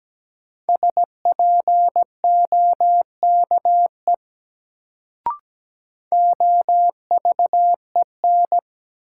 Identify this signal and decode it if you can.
Morse code, heard as a keyed tone, about 17 wpm, 700 Hz. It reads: SPOKE OVEN
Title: Sentences from Top 1200 Words - New Word Drill Alt - 17wpm